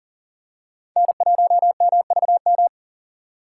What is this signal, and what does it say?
Morse code, heard as a keyed tone, about 40 wpm, 690 Hz. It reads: N1MVM
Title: USA Call Signs - 2x Character Spacing - 40wpm